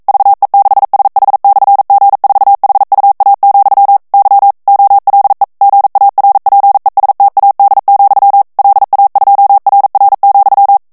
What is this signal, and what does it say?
Morse code, heard as a keyed tone, about 42 wpm, 800 Hz. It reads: VE6SH/G4HUA, Y OLE GARPESTAD, LA2RR,
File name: morse mp3 / ESPAGNOL TEXTE EN CLAIR 1 VITESSE 40 MOTS MINUTES